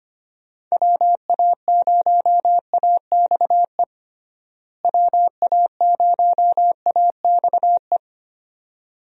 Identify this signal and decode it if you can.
Morse code, heard as a keyed tone, about 25 wpm, 700 Hz. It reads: WA0AXE WA0AXE
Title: USA Call Signs - 6 Characters - Repeated 2x - 25wpm